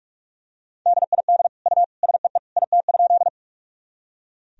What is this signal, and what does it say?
Morse code, heard as a keyed tone, about 46 wpm, 700 Hz. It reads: DID U SEE IT?
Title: Sentences from Top 100 Words and Abbreviations 45wpm